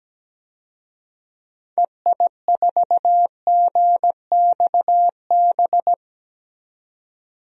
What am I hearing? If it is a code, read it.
Morse code, heard as a keyed tone, about 17 wpm, 700 Hz. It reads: EI4GXB
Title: DXCC-100 Call Signs - 6 Characters - 17wpm